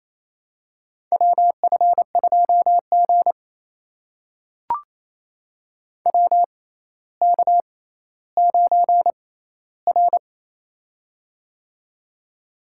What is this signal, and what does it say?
Morse code, heard as a keyed tone, about 27 wpm, spaced as for 14 wpm, 700 Hz. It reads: WF2G WK9R
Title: USA Call Signs - 4 Characters - 6x Character Spacing - 28wpm